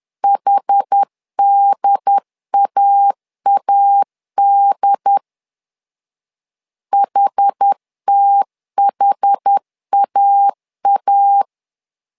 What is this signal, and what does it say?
Morse code, heard as a keyed tone, about 10 wpm, 800 Hz. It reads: HDAAD HTHAA
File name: LEZIONI / 2 UHDN / 2 UHD 10wpm 1826